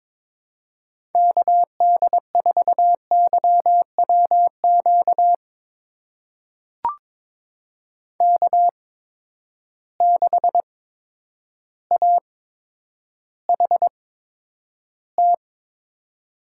Morse code, heard as a keyed tone, about 22 wpm, spaced as for 10 wpm, 700 Hz. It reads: KD4YWQ K6AHT
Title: USA Call Signs - 8x Character Spacing - 22wpm